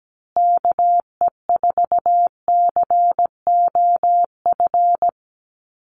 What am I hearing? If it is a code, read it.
Morse code, heard as a keyed tone, about 17 wpm, 700 Hz. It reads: KE4COF
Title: USA Call Signs Repeated 3x 17wpm